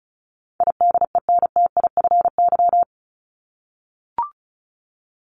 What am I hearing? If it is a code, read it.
Morse code, heard as a keyed tone, about 34 wpm, 700 Hz. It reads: IDENTIFY
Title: Top 751-1000 Words 35wpm